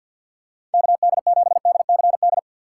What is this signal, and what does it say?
Morse code, heard as a keyed tone, about 53 wpm, 700 Hz. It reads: KD7DCD
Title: USA Call Signs - 6 Characters - 6x Character Spacing - 50wpm